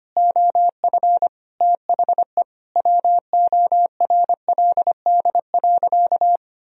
Morse code, heard as a keyed tone, about 25 wpm, 700 Hz. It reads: OF THE WORLD.
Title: Sentences from Top 1000 Words - Review Part 1 - 25wpm